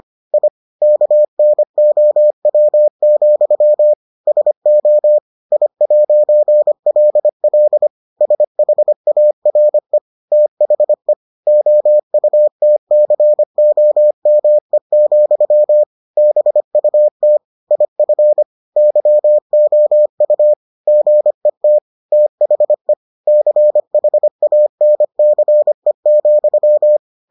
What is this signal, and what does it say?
Morse code, heard as a keyed tone, about 25 wpm, 600 Hz. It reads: I KNOW, SO I'LL SHARE THE OUTCOME, BUT IF YOU GET THE CHANCE,